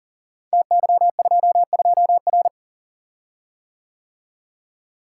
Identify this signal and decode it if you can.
Morse code, heard as a keyed tone, about 39 wpm, 700 Hz. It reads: TY22R
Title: Code Groups - 5 Letter:Numbers - 40wpm